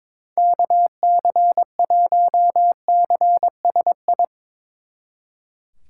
Morse code, heard as a keyed tone, about 26 wpm, 700 Hz. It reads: KC1CSI
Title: Speed Racing - USA 2x3 Call Signs 22wpm